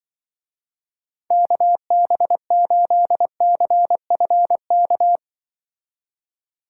Morse code, heard as a keyed tone, about 24 wpm, 700 Hz. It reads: KB8CFK
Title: Speed Racing - USA 2x3 Call Signs 20wpm